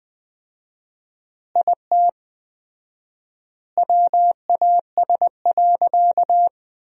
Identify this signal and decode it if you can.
Morse code, heard as a keyed tone, about 20 wpm, 700 Hz. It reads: IT WAS.
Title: 4x Word-Spacing - Sentences from Top 100 Words - 20wpm